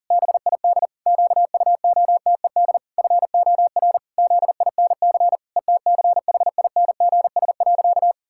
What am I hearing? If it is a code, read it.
Morse code, heard as a keyed tone, about 40 wpm, 700 Hz. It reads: BID QUOTED FOR ZINC ETCHINGS.
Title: Pangram Challenge v2 - 40wpm